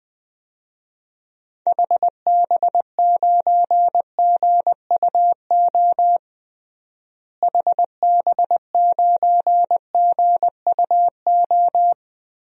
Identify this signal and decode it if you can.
Morse code, heard as a keyed tone, about 20 wpm, 700 Hz. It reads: HB9GUO HB9GUO
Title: DXCC-100 Call Signs - 6 Characters - Repeated 3x - 20wpm